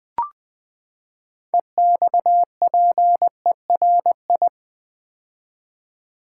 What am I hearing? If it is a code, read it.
Morse code, heard as a keyed tone, about 20 wpm, 700 Hz. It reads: EXPERI
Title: Build A Word - 6 Letters At A Time - 12-Letter Words - 20wpm